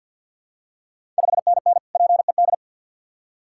Morse code, heard as a keyed tone, about 46 wpm, 690 Hz. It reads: 5NN PED
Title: Ontario QSO Party Exchange 50wpm